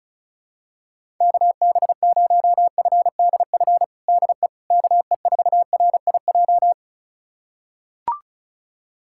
Morse code, heard as a keyed tone, about 36 wpm, 700 Hz. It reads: KB0FDF DE KE4RIJ
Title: USA Callsign DE Callsign 35wpm